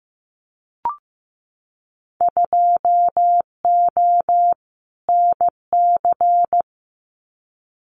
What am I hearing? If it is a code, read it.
Morse code, heard as a keyed tone, about 15 wpm, 700 Hz. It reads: 2O NC
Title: Winter Field Day Exchanges 15wpm